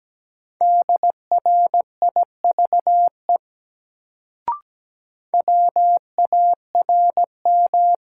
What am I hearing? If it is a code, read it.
Morse code, heard as a keyed tone, about 17 wpm, 700 Hz. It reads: DRIVE WARM
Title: Sentences from Top 500 Words - New Word Drill - 17wpm